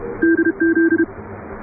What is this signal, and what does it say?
Morse code, heard as a keyed tone, about 31 wpm, 340 Hz. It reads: DZ
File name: chatham / 0216-15CW